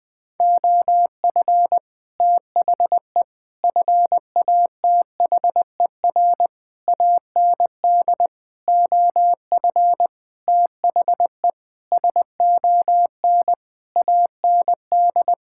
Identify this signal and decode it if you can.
Morse code, heard as a keyed tone, about 20 wpm, 700 Hz. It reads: OF THE FATHER AND OF THE SON AND
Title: Sentences with 9-Word Phrases - Spoken Beginning - 20wpm